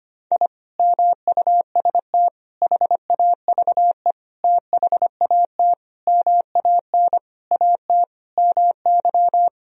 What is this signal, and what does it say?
Morse code, heard as a keyed tone, about 25 wpm, 700 Hz. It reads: I MUST HAVE THAT MAN AT MY
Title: Sentences from Top 1000 Words - Review Part 2 - 25wpm